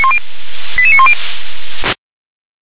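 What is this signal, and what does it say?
Five-tone selective call (ZVEI), heard as quick runs of five tones, tones of about 70 ms. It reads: 88910
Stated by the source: ZVEI five-tone selective call received on an SDR.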